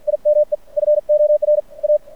Morse code, heard as a keyed tone, about 50 wpm, 600 Hz. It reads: IME FOR A
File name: archived / CRSnet